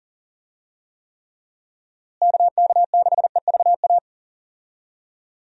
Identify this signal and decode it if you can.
Morse code, heard as a keyed tone, about 40 wpm, 700 Hz. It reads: KK6EVA
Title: USA Call Signs - 6 Characters - Repeated 2x - 40wpm